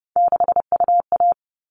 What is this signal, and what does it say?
Morse code, heard as a keyed tone, about 30 wpm, 700 Hz. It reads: 6UA
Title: All Cut Numbers - 3-Digit 30wpm